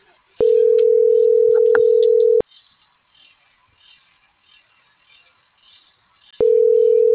Ringback tone (North America), 440 + 480 Hz.